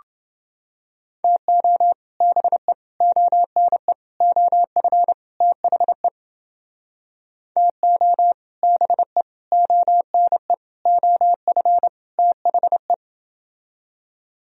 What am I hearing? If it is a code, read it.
Morse code, heard as a keyed tone, about 30 wpm, 700 Hz. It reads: TO BE ONE OF THE TO BE ONE OF THE